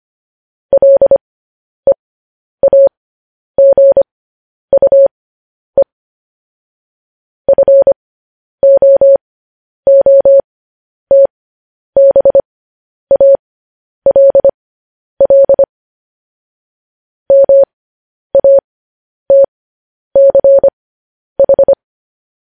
Morse code, heard as a keyed tone, about 25 wpm, spaced as for 10 wpm, 550 Hz. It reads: LEAGUE FOOTBALL MATCH
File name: podcast / News headlines-1x-25x10